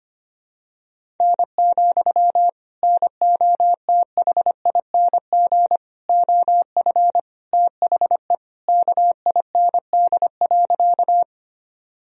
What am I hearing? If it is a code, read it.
Morse code, heard as a keyed tone, about 25 wpm, 700 Hz. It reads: N, NOTHING OF THE KIND.
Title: Sentences from Top 500 Words and Abbreviations 25wpm